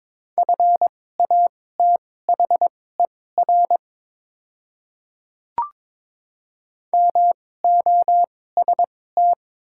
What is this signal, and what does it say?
Morse code, heard as a keyed tone, about 22 wpm, spaced as for 16 wpm, 690 Hz. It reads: FATHER MOST
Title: 2x Character Spacing - Top 200 Words - 22wpm